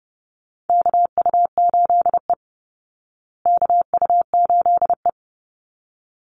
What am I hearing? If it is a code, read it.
Morse code, heard as a keyed tone, about 30 wpm, 700 Hz. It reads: KU8E KU8E
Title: Call Signs Repeated 3x - 30wpm